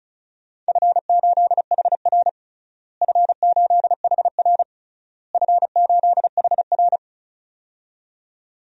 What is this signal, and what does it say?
Morse code, heard as a keyed tone, about 36 wpm, 700 Hz. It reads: F8HR F8HR F8HR
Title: DXCC-100 Call Signs - 4 Characters - Repeated 3x - 35wpm